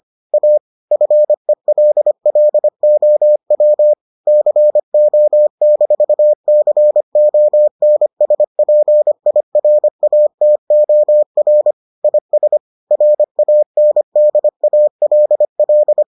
Morse code, heard as a keyed tone, about 25 wpm, 600 Hz. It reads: A FELLOW CO-CONSPIRATOR IS RANDALL